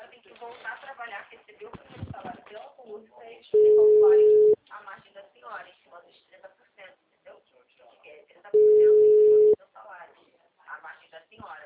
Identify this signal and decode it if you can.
Ringback tone (Europe, 425 Hz), with speech.
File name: out- -2016- - - .47450